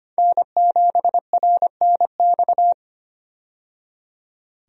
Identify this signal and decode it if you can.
Morse code, heard as a keyed tone, about 25 wpm, 700 Hz. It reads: N7RNX
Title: USA Call Signs - 5 Characters - 25wpm